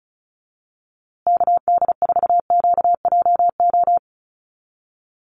Morse code, heard as a keyed tone, about 34 wpm, 700 Hz. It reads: KD4QJO
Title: USA Call Signs Repeated 2x 35wpm